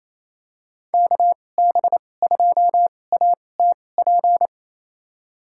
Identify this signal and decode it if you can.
Morse code, heard as a keyed tone, about 28 wpm, spaced as for 20 wpm, 700 Hz. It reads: KB2ATP
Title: USA Call Signs - 6 Characters - 2x Character Spacing - 28wpm